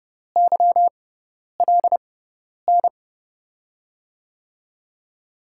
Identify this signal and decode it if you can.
Morse code, heard as a keyed tone, about 30 wpm, spaced as for 10 wpm, 700 Hz. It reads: YLN
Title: USA Call Signs - 6 Characters - 6x Character Spacing - 30wpm